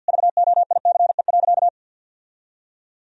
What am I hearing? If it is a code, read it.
Morse code, heard as a keyed tone, about 52 wpm, 700 Hz. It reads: VOICE.
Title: Sentences from Top 1000 Words - Review Part 1 - 50wpm